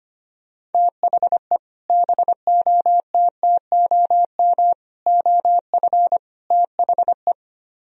Morse code, heard as a keyed tone, about 32 wpm, 690 Hz. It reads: THE BOTTOM OF THE